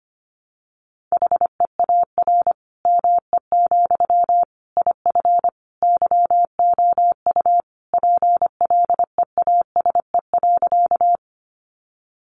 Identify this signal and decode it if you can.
Morse code, heard as a keyed tone, about 25 wpm, 700 Hz. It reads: HEAR ME, IF YOU PLEASE.
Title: Sentences from Top 1500 Words 25wpm